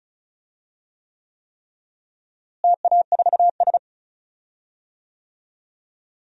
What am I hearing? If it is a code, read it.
Morse code, heard as a keyed tone, about 36 wpm, 700 Hz. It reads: TA4S
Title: DXCC-100 Call Signs - 4 Characters - Repeated 3x - 35wpm